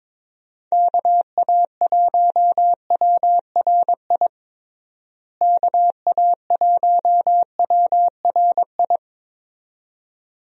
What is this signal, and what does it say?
Morse code, heard as a keyed tone, about 21 wpm, 700 Hz. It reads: KA1WRI KA1WRI
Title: USA Call Signs - 6 Characters - Repeated 2x - 22wpm